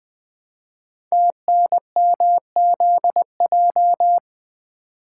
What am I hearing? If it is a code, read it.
Morse code, heard as a keyed tone, about 20 wpm, 700 Hz. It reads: TNMZJ
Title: Code Groups - 5 Letters - 20wpm